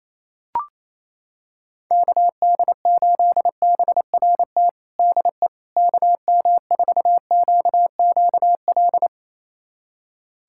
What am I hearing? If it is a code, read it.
Morse code, heard as a keyed tone, about 28 wpm, 700 Hz. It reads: KD8BRT DE KM4QQL
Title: USA Callsign DE Callsign 28wpm